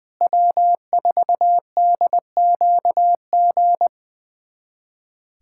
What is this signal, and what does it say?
Morse code, heard as a keyed tone, about 20 wpm, 700 Hz. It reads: W4DQG
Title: USA Callsign DE Callsign - 5 Characters - 20wpm